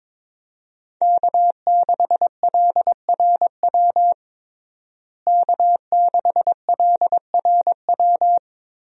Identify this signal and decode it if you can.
Morse code, heard as a keyed tone, about 22 wpm, 700 Hz. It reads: K6LRW K6LRW
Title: USA Call Signs - 5 Characters - Repeated 2x - 22wpm